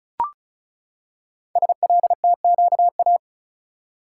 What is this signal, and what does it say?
Morse code, heard as a keyed tone, about 36 wpm, 700 Hz. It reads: SLTQA
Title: Code Groups - 5 Letters - 35wpm